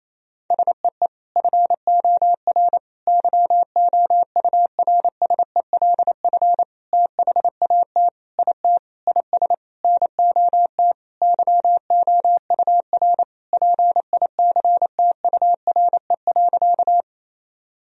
Morse code, heard as a keyed tone, about 28 wpm, 700 Hz. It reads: SEE FOR YOURSELF THAT IT IS NOT YOUR PICTURE.
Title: Sentences from Top 1000 Words - Review Part 1 - 28wpm